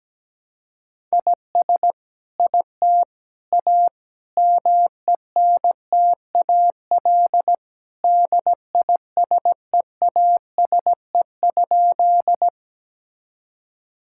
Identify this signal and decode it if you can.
Morse code, heard as a keyed tone, about 17 wpm, 700 Hz. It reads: IS IT A MENTAL DISEASE?
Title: Sentences from Top 1000 Words - Review Part 6 - 17wpm